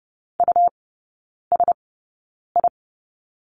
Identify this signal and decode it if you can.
Morse code, heard as a keyed tone, about 30 wpm, spaced as for 9 wpm, 700 Hz. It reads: USI
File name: Lesson 18e - Letter m - Three Characters 30wpm